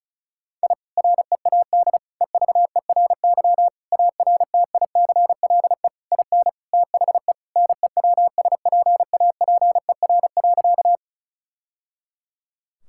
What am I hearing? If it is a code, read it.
Morse code, heard as a keyed tone, about 36 wpm, 700 Hz. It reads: I READ EVERY ARTICLE IN THE NEWSPAPER.
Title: Sentences from Top 1000 Words - Review Part 5 - 35wpm